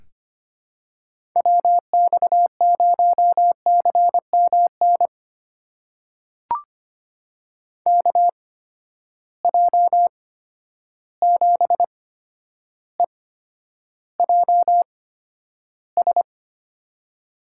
Morse code, heard as a keyed tone, about 25 wpm, spaced as for 11 wpm, 700 Hz. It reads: WX0CMN KJ7EJS
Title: USA Call Signs - 6 Characters - 8x Character Spacing - 25wpm